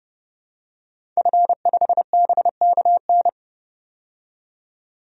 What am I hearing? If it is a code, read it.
Morse code, heard as a keyed tone, about 32 wpm, 700 Hz. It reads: F5BKN
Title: DXCC-100 Call Signs - 5 Characters - 30wpm